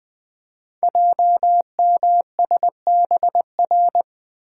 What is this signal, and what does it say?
Morse code, heard as a keyed tone, about 20 wpm, 700 Hz. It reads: JMSBR
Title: Code Groups - 5 Letters - 20wpm